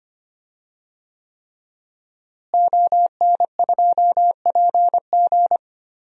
Morse code, heard as a keyed tone, about 25 wpm, 700 Hz. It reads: ON2PG
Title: DXCC-100 Call Signs - 5 Characters - Repeated 3x - 25wpm